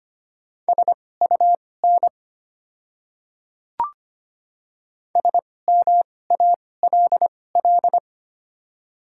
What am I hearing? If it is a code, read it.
Morse code, heard as a keyed tone, about 25 wpm, spaced as for 18 wpm, 690 Hz. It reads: SUN SMALL
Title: 2x Character Spacing - Top 300 Words - 25wpm